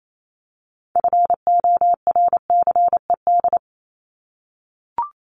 Morse code, heard as a keyed tone, about 28 wpm, 700 Hz. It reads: FORCED
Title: ED Suffix - Top 100 Words with Base Words 28wpm